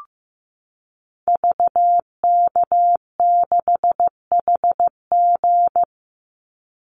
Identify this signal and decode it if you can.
Morse code, heard as a keyed tone, about 14 wpm, 700 Hz. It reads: VK6HG
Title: Call Signs Repeated 3x - 15wpm